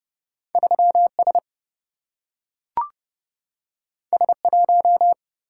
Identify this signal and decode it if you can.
Morse code, heard as a keyed tone, about 30 wpm, 700 Hz. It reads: 3 S S 1
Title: Single-Letter Single-Number 30wpm